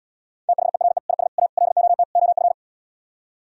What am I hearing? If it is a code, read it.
Morse code, heard as a keyed tone, about 77 wpm, 700 Hz. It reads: THERE IS A LOT OF